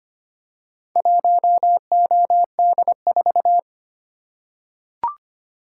Morse code, heard as a keyed tone, about 25 wpm, 700 Hz. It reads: 1OD4
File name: practice / 25wpm / Four Letter-Number Combinations - 25wpm